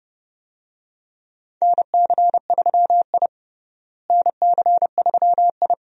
Morse code, heard as a keyed tone, about 30 wpm, 700 Hz. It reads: NC3I NC3I
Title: USA Call Signs - 4 Characters - Repeated 2x - 30wpm